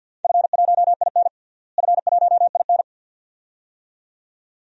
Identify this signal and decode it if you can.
Morse code, heard as a keyed tone, about 49 wpm, 700 Hz. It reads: F1IN F1IN
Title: DXCC-100 Call Signs - 4 Characters - Repeated 2x - 50wpm